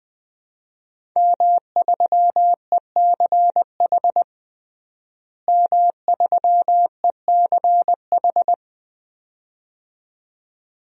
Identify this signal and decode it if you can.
Morse code, heard as a keyed tone, about 20 wpm, 700 Hz. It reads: M3ECH M3ECH
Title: DXCC-100 Call Signs - 5 Characters - Repeated 2x - 20wpm